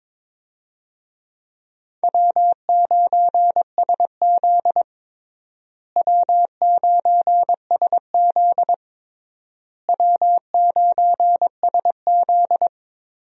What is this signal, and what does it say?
Morse code, heard as a keyed tone, about 22 wpm, 700 Hz. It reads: W9SZ W9SZ W9SZ
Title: USA Call Signs - 4 Characters - Repeated 3x - 22wpm